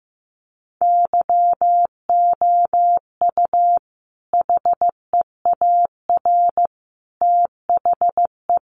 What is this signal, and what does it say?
Morse code, heard as a keyed tone, about 15 wpm, 700 Hz. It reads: YOU HEAR THE
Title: Sentences from Top 900 Words 15wpm